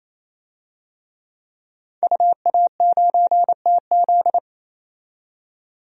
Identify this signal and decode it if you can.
Morse code, heard as a keyed tone, about 27 wpm, 700 Hz. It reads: UA9TZ
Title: DXCC-100 Call Signs - 5 Characters - 28wpm